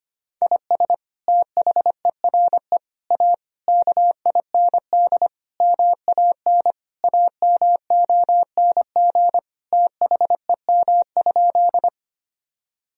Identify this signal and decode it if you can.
Morse code, heard as a keyed tone, about 25 wpm, 700 Hz. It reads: IS THERE A KIND MAN AMONG THEM?
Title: Sentences from Top 1000 Words - Review Part 3 - 25wpm